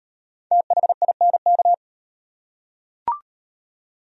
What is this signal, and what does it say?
Morse code, heard as a keyed tone, about 37 wpm, 700 Hz. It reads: THINK